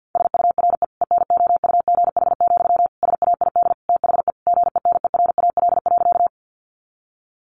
Morse code, heard as a keyed tone, about 50 wpm, 700 Hz. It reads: SURE ENOUGH, SAID THE GENERAL.
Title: Sentences from Top 600 Words 50wpm